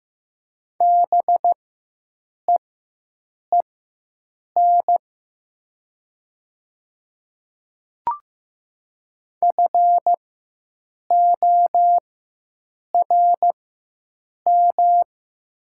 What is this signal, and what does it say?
Morse code, heard as a keyed tone, about 15 wpm, spaced as for 7 wpm, 700 Hz. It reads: BEEN FORM
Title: 4x Character Spacing - Top 300 Words - 15wpm